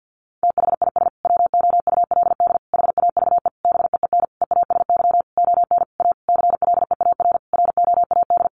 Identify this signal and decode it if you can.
Morse code, heard as a keyed tone, about 50 wpm, 700 Hz. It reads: THIS WOULD HAVE BEEN EASY ON A CLEAR ROAD
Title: Sentences from Top 800 Words 50wpm